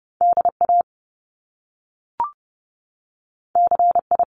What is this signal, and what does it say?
Morse code, heard as a keyed tone, about 30 wpm, 700 Hz. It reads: DA CI
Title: Top Two Letter Combinations in Words 30wpm